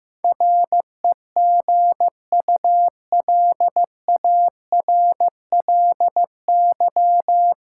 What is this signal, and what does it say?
Morse code, heard as a keyed tone, about 15 wpm, 700 Hz. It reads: REGULARLY
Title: Sentences from Top 1400 Words - New Word Drill - 15wpm